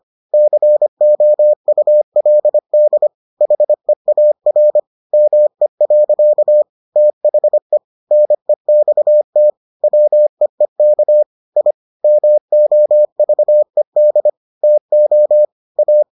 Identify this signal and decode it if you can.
Morse code, heard as a keyed tone, about 25 wpm, 600 Hz. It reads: COULD HEAR ME. THE NEXT WEEK I MOVED TO A